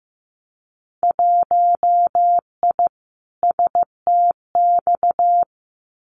Morse code, heard as a keyed tone, about 15 wpm, 700 Hz. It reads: 1I STX
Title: Winter Field Day Exchanges 15wpm